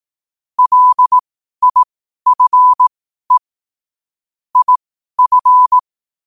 Morse code, heard as a keyed tone, about 18 wpm, spaced as for 12 wpm, 1000 Hz. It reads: LIFE IF
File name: qotdfiles / 12wpm